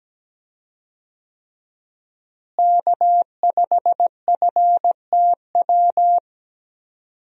Morse code, heard as a keyed tone, about 17 wpm, 700 Hz. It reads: K5FTW
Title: USA Call Signs - 5 Characters - Repeated 3x - 17wpm